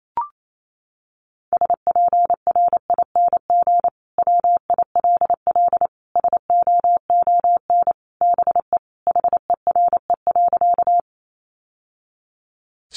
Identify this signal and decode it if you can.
Morse code, heard as a keyed tone, about 28 wpm, 700 Hz. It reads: SPRING WILL SOON BE HERE.
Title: Sentences from Top 1200 Words 28wpm